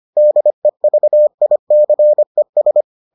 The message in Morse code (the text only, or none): DEVICES